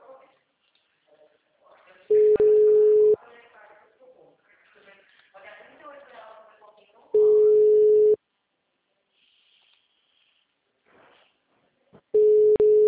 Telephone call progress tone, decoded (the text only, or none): ringback tone